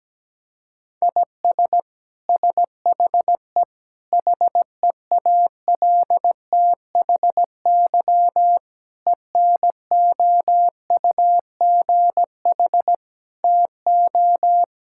Morse code, heard as a keyed tone, about 17 wpm, 700 Hz. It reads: IS SHE HEALTHY ENOUGH TO
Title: Sentences from Top 1000 Words - Review Part 5 - 17wpm